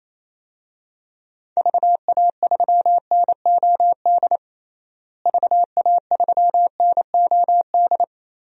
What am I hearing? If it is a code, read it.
Morse code, heard as a keyed tone, about 27 wpm, 700 Hz. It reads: VA3NOD VA3NOD
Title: DXCC-100 Call Signs - 6 Characters - Repeated 2x - 28wpm